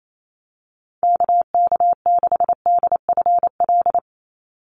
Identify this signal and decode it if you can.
Morse code, heard as a keyed tone, about 28 wpm, 700 Hz. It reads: KK6DFL
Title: USA Call Signs 28wpm